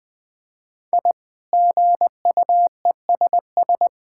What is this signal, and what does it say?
Morse code, heard as a keyed tone, about 21 wpm, 700 Hz. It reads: I GUESS
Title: Sets of 8 Words - 20wpm